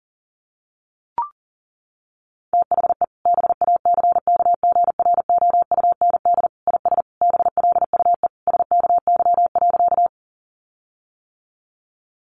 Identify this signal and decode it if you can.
Morse code, heard as a keyed tone, about 40 wpm, 700 Hz. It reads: THE BACKGROUND IS BLUE SKY.
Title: Sentences from Top 1300 Words 40wpm